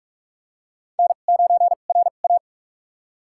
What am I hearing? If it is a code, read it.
Morse code, heard as a keyed tone, about 46 wpm, spaced as for 33 wpm, 700 Hz. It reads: N9RA
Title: USA Call Signs - 4 Characters - 2x Character Spacing - 45wpm